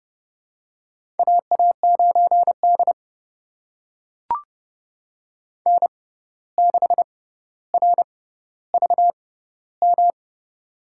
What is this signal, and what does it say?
Morse code, heard as a keyed tone, about 30 wpm, spaced as for 14 wpm, 700 Hz. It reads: AA9D N6RVM
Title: USA Call Signs - 6x Character Spacing - 30wpm